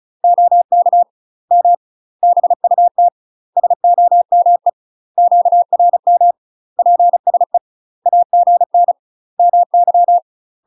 Morse code, heard as a keyed tone, about 36 wpm, 700 Hz. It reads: OK M BUT SOME QRM PSE AGN MY
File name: UK MASTER QSOs TRAINING / 35WPM / UK MASTER QSOs 0037